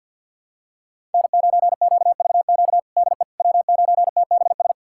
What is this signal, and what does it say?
Morse code, heard as a keyed tone, about 47 wpm, 700 Hz. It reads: N9QVQ DE W9TBS
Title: USA Callsign DE Callsign - 5 Characters - 50wpm